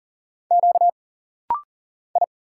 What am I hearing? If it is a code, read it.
Morse code, heard as a keyed tone, about 40 wpm, 700 Hz. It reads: Q I
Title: Single Letters - Rapid-Fire 40wpm